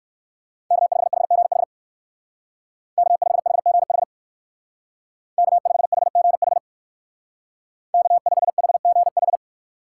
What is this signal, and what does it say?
Morse code, heard as a keyed tone, about 56 wpm, 700 Hz. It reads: K5HGH K5HGH K5HGH K5HGH
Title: Speed Racing - USA Call Signs 45wpm